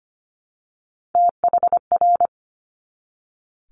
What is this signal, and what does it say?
Morse code, heard as a keyed tone, about 25 wpm, 700 Hz. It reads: THR
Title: Multi-letter Phonemes 25wpm